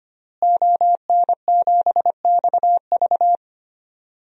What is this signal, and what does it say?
Morse code, heard as a keyed tone, about 25 wpm, 700 Hz. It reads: ON7XV
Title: DXCC-100 Call Signs - 5 Characters - Repeated 2x - 25wpm